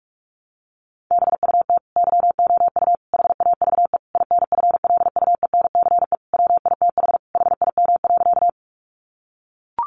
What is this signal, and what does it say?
Morse code, heard as a keyed tone, about 44 wpm, 700 Hz. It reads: BUT YOU HAVE INFLUENCE WITH HIM.
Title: Sentences from Top 1100 Words 45wpm